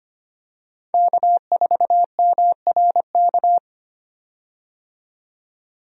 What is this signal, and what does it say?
Morse code, heard as a keyed tone, about 25 wpm, 690 Hz. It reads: K4MRK
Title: USA Call Signs - 5 Characters - 25wpm